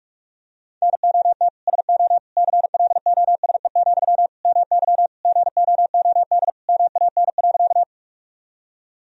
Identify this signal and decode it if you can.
Morse code, heard as a keyed tone, about 42 wpm, 700 Hz. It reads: NOT SO CLOSE, MY GOOD MAN.
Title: Sentences from Top 1000 Words - Review Part 1 - 45wpm